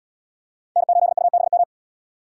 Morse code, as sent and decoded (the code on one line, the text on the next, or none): -. ---.. ..- -... --
N8UBM